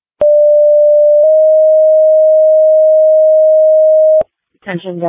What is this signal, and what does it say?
Two-tone page, heard as a one-second tone, then a three-second tone, about 590 then 620 Hz, with speech. It is heard from 0.2 to 4.2 s.
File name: Denver Ambulance 2025 03 06 10 50 21